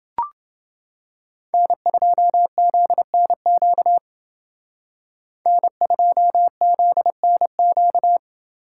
Morse code, heard as a keyed tone, about 29 wpm, 700 Hz. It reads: N2ZNQ N2ZNQ